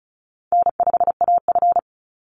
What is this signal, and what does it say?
Morse code, heard as a keyed tone, about 34 wpm, 700 Hz. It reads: N5AF
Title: Easy Call Signs Repeated 2x - 35wpm